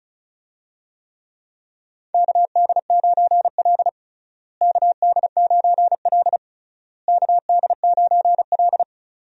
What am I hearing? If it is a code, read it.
Morse code, heard as a keyed tone, about 34 wpm, 700 Hz. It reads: KD9L KD9L KD9L
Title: USA Call Signs - 4 Characters - Repeated 3x - 35wpm